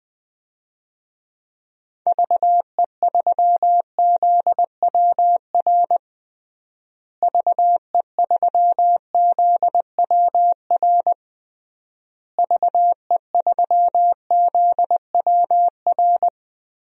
Morse code, heard as a keyed tone, about 20 wpm, 700 Hz. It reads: VE3ZWR VE3ZWR VE3ZWR
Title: DXCC-100 Call Signs - 6 Characters - Repeated 3x - 20wpm